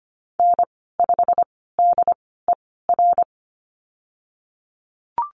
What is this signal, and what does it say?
Morse code, heard as a keyed tone, about 25 wpm, spaced as for 16 wpm, 700 Hz. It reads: N5DER
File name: Lesson 15g - Number 5 - Call Signs 25wpm